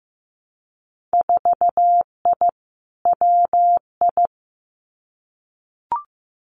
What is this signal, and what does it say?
Morse code, heard as a keyed tone, about 15 wpm, 700 Hz. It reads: 4I WI
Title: Winter Field Day Exchanges 15wpm